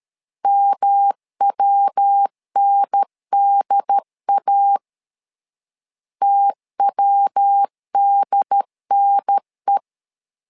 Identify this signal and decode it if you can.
Morse code, heard as a keyed tone, about 13 wpm, 800 Hz. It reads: MWNDA TWDNE
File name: LEZIONI / 3 MWRO / 3 MW 12wpm 1633